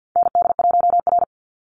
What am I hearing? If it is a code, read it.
Morse code, heard as a keyed tone, about 50 wpm, 700 Hz. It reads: ND1R
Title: Call Signs - 50wpm